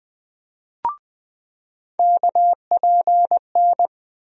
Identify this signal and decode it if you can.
Morse code, heard as a keyed tone, about 20 wpm, 700 Hz. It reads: KPN
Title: USA Call Sign Postfixes - 3 Letters 20wpm